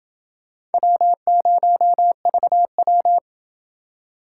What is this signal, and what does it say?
Morse code, heard as a keyed tone, about 29 wpm, 700 Hz. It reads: W0VW